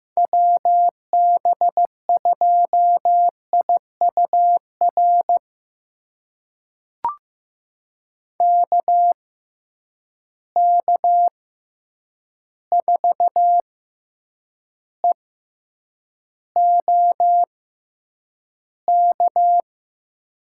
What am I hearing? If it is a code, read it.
Morse code, heard as a keyed tone, about 15 wpm, spaced as for 8 wpm, 700 Hz. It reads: WB2IUR KK4EOK
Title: USA Call Signs - 6 Characters - 6x Character Spacing - 15wpm